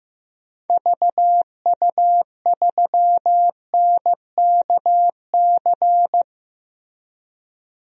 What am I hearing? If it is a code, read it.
Morse code, heard as a keyed tone, about 15 wpm, 700 Hz. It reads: VU3NKC
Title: DXCC-100 Call Signs - 6 Characters - Repeated 3x - 15wpm